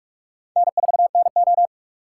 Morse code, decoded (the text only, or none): N4NO